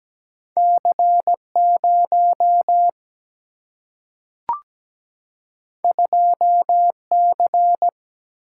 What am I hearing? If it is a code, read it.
Morse code, heard as a keyed tone, about 17 wpm, 700 Hz. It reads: C0 2C
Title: Single-Letter Single-Number 17wpm